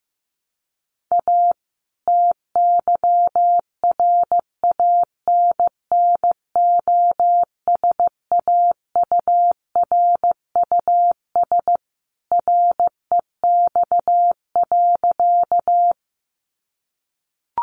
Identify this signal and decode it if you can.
Morse code, heard as a keyed tone, about 15 wpm, 700 Hz. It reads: A TYRANNOSAURUS REX.
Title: A Sound Of Thunder 15wpm